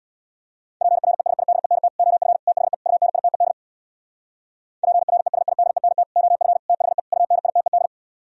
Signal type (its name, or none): Morse code